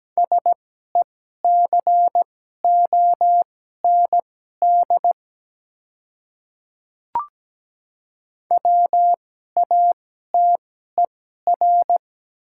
Morse code, heard as a keyed tone, about 17 wpm, spaced as for 12 wpm, 690 Hz. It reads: SECOND WATER
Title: 2x Character Spacing - Top 300 Words - 17wpm